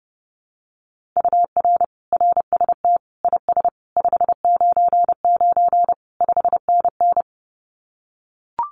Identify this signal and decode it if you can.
Morse code, heard as a keyed tone, about 30 wpm, 700 Hz. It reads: UR RST IS 599 5NN
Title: Common QSO Phrases Encore 30wpm